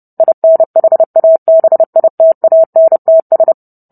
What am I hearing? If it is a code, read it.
Morse code, heard as a keyed tone, about 30 wpm, 650 Hz. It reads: INHABITANTS